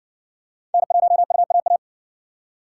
Morse code, heard as a keyed tone, about 60 wpm, 700 Hz. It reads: N1UAA